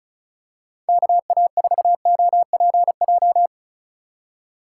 Morse code, heard as a keyed tone, about 36 wpm, 700 Hz. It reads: KA4OPJ
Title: USA Call Signs - 6 Characters - 4x Character Spacing - 35wpm